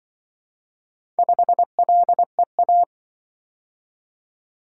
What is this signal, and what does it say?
Morse code, heard as a keyed tone, about 24 wpm, 700 Hz. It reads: 5LEA
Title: Speed-Racing - Four Letter-Number Combinations - 20wpm